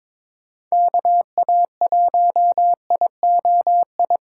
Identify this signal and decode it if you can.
Morse code, heard as a keyed tone, about 22 wpm, 700 Hz. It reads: KA1IOI
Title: USA Call Signs - 6 Characters - 6x Character Spacing - 22wpm